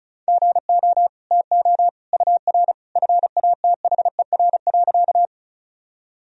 Morse code, heard as a keyed tone, about 36 wpm, 700 Hz. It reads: GO TO UR FATHER.
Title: Sentences from Top 300 Words and Abbreviations 35wpm